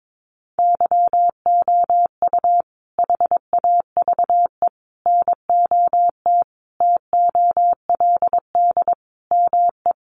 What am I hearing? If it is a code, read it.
Morse code, heard as a keyed tone, about 22 wpm, 700 Hz. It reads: YOU HAVE NOT TOLD ME
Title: Sentences from Top 1000 Words 22wpm